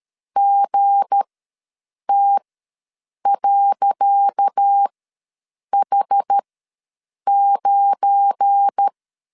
Morse code, heard as a keyed tone, about 13 wpm, spaced as for 7 wpm, 800 Hz. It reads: GT.H9
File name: LEZIONI / 10 SIGNS / 10C DOT ws 12wpm 3548